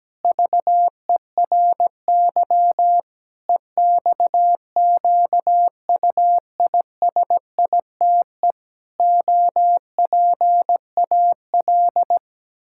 Morse code, heard as a keyed tone, about 17 wpm, 700 Hz. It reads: VERY EXQUISITE OPAL
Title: Pangram Challenge v2 - 17wpm